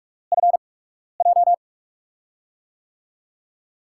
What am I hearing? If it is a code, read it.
Morse code, heard as a keyed tone, about 46 wpm, spaced as for 12 wpm, 700 Hz. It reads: FJ
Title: USA Call Signs - 6 Characters - 8x Character Spacing - 45wpm